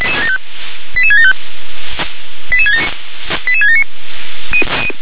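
Five-tone selective call (ZVEI), heard as quick runs of five tones, tones of about 70 ms. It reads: 88675 88679
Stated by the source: ZVEI five-tone selective call received by SDR.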